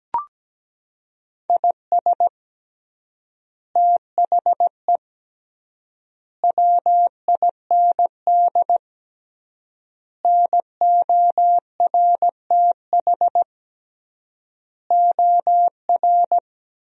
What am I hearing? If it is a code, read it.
Morse code, heard as a keyed tone, about 17 wpm, 700 Hz. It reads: IS THE WIND NORTH OR
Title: 3x Word-Spacing - Sentences from Top 400 Words - 17wpm